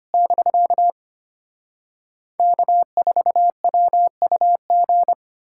Morse code, heard as a keyed tone, about 33 wpm, 700 Hz. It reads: <BK> K4WUG